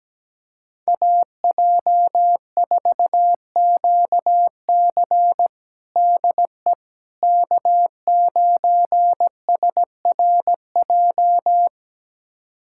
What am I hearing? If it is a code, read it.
Morse code, heard as a keyed tone, about 17 wpm, 700 Hz. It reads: AJ4QC DE K9SRJ
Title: USA Callsign DE Callsign - 5 Characters - 17wpm